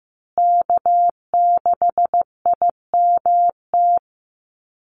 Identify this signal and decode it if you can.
Morse code, heard as a keyed tone, about 15 wpm, 700 Hz. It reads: K6IMT
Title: USA Call Signs Repeated 3x 15wpm